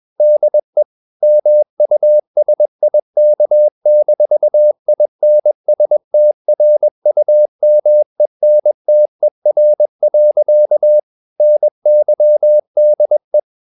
Morse code, heard as a keyed tone, about 21 wpm, 600 Hz. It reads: DE MUSIK-INSTRUMENTER. NYDE
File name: Ferie QRS 21wpm